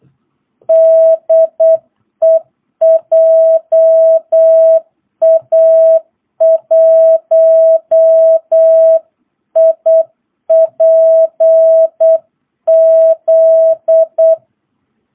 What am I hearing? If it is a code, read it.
Morse code, heard as a keyed tone, about 8 wpm, 650 Hz. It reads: DE JA1IPZ
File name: jh8cbh / 2022cw